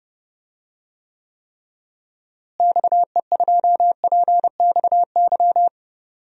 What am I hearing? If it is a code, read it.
Morse code, heard as a keyed tone, about 32 wpm, 700 Hz. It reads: XE2PXY
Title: DXCC-100 Call Signs - 6 Characters - Repeated 3x - 30wpm